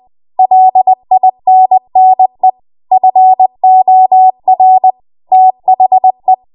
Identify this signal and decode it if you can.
Morse code, heard as a keyed tone, about 20 wpm, 750 Hz. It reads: LINNE FOR THE
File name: Morse / 20WPM